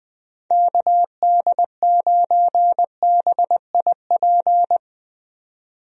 Morse code, heard as a keyed tone, about 21 wpm, 700 Hz. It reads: KD9BIP
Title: USA Call Signs - 6 Characters - 20wpm